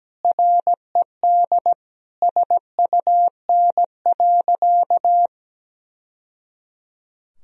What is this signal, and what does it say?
Morse code, heard as a keyed tone, about 17 wpm, 700 Hz. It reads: RED SUN.
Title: Sentences from Top 1000 Words - Review Part 2 - 17wpm